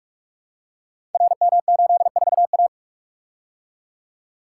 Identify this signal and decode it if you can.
Morse code, heard as a keyed tone, about 44 wpm, 700 Hz. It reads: RM84A
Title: Code Groups - 5 Letter:Numbers - 45wpm